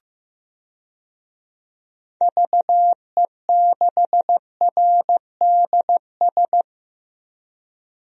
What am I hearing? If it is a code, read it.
Morse code, heard as a keyed tone, about 15 wpm, 700 Hz. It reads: VE6RDS
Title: DXCC-100 Call Signs - 6 Characters - 15wpm